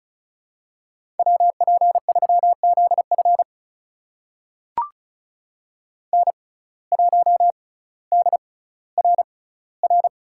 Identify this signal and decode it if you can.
Morse code, heard as a keyed tone, about 36 wpm, spaced as for 18 wpm, 700 Hz. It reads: WP3ZF N1DRR
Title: USA Call Signs - 5 Characters - 6x Character Spacing - 35wpm